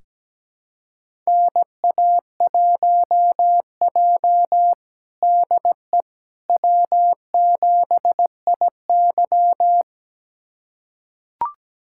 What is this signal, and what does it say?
Morse code, heard as a keyed tone, about 17 wpm, 700 Hz. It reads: NA1J DE W7IY
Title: USA Callsign DE Callsign - 4 Characters - 17wpm